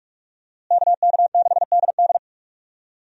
Morse code, heard as a keyed tone, about 42 wpm, 700 Hz. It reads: KK6DD
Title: USA Call Signs - 5 Characters - Repeated 2x - 45wpm